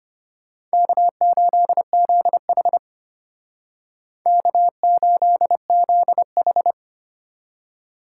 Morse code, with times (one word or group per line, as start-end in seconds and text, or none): 0.7-2.8 K8ZH
4.3-6.7 K8ZH